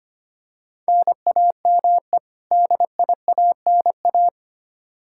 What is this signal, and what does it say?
Morse code, heard as a keyed tone, about 25 wpm, 700 Hz. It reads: NAME DIANA
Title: Practice QSO - Exchange 1 - Beginning - 25wpm